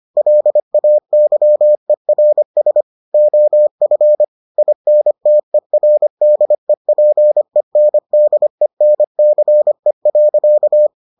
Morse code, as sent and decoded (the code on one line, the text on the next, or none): .-.. .- -.-- . .-. ... / --- ..-. / .. -. - . .-. -.. .--. -. -.. -. -.-. .-.-.-
LAYERS OF INTERDEPENDENCE.